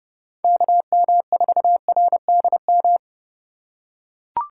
Morse code, heard as a keyed tone, about 29 wpm, 700 Hz. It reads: KM4RDM